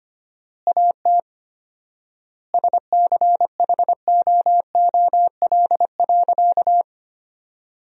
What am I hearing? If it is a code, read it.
Morse code, heard as a keyed tone, about 25 wpm, 700 Hz. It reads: AT SCHOOL.
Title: 4x Word-Spacing - Sentences from Top 300 Words - 25wpm